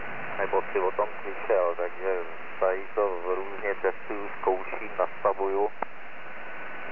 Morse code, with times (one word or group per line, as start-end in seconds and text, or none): none